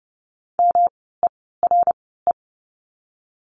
Morse code, heard as a keyed tone, about 30 wpm, spaced as for 17 wpm, 700 Hz. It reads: MERE
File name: course / 30:10wpm / 108 Lesson 18f - Letter m - Words 30wpm